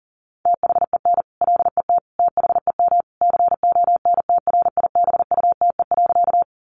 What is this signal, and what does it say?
Morse code, heard as a keyed tone, about 37 wpm, 700 Hz. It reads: THEN LET THEM CONTRIBUTE.
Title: Sentences from Top 1400 Words 40wpm